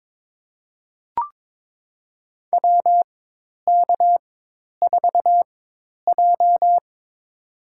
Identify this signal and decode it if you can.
Morse code, heard as a keyed tone, about 22 wpm, spaced as for 10 wpm, 700 Hz. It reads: WK4J
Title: USA Call Signs - 4x Character Spacing - 22wpm